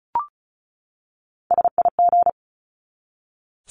Morse code, heard as a keyed tone, about 34 wpm, 700 Hz. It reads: SIG
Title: Abbreviations 35wpm